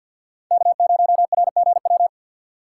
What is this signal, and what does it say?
Morse code, heard as a keyed tone, about 52 wpm, 700 Hz. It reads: K0RGW
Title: USA Call Signs - 5 Characters - 4x Character Spacing - 50wpm